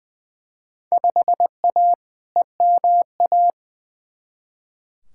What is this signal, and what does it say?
Morse code, heard as a keyed tone, about 20 wpm, 700 Hz. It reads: 5A EMA
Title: Field Day Exchanges - 20wpm